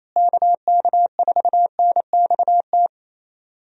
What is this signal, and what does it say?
Morse code, heard as a keyed tone, about 27 wpm, 700 Hz. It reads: KK4NXT
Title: USA Call Signs - 6 Characters - Repeated 2x - 28wpm